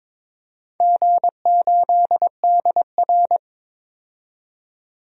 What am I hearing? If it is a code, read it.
Morse code, heard as a keyed tone, about 23 wpm, 700 Hz. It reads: G8DR
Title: DXCC-100 Call Signs - 4 Characters - 22wpm